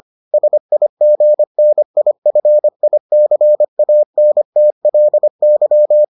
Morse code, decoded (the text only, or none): SIGNIFICANTLY